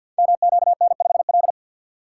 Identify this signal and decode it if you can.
Morse code, heard as a keyed tone, about 52 wpm, 700 Hz. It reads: MQN5L
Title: Code Groups - 5 Letter:Numbers - 50wpm